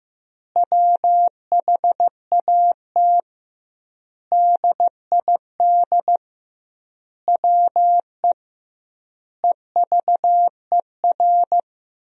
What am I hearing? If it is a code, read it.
Morse code, heard as a keyed tone, about 16 wpm, 700 Hz. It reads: WHAT DID WE EVER
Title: 2x Word-Spacing - Sentences from Top 400 Words - 15wpm